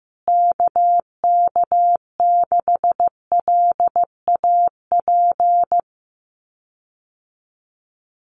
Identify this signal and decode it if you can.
Morse code, heard as a keyed tone, about 15 wpm, 700 Hz. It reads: KK6LAP
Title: USA Call Signs Repeated 2x 15wpm